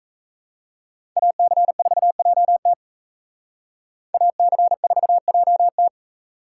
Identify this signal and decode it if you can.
Morse code, heard as a keyed tone, about 35 wpm, 700 Hz. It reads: AC4JT AC4JT
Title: Speed Racing - USA 2x2 Call Signs 28wpm